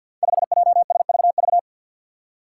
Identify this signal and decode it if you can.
Morse code, heard as a keyed tone, about 51 wpm, 700 Hz. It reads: 5JSVV
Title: Five Challenging Letter-Number Combinations 50wpm